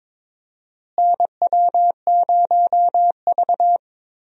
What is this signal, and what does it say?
Morse code, heard as a keyed tone, about 21 wpm, 700 Hz. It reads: NW0V
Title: USA Call Signs - 4 Characters - Repeated 2x - 22wpm